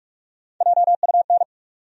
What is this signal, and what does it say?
Morse code, heard as a keyed tone, about 46 wpm, 700 Hz. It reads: JUN
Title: USA Call Sign Postfixes - 3 Letters 45wpm